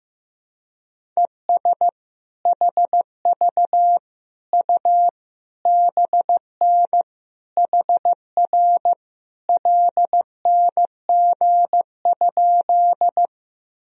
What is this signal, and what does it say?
Morse code, heard as a keyed tone, about 15 wpm, 700 Hz. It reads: ES HV U BN HR LNG?
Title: Sentences from Top 200 Words and Abbreviations 15wpm